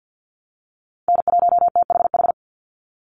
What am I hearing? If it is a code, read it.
Morse code, heard as a keyed tone, about 50 wpm, 700 Hz. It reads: N1THH
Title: USA Call Signs 50wpm